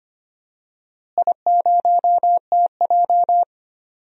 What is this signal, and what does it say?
Morse code, heard as a keyed tone, about 25 wpm, 700 Hz. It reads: I0TJ
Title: DXCC-100 Call Signs - 4 Characters - Repeated 3x - 25wpm